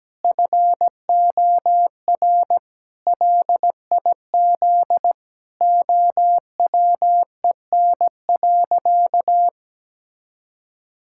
Morse code, heard as a keyed tone, about 17 wpm, 700 Hz. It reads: FOR LIZ OWEN.
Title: Pangram Challenge v2 - 17wpm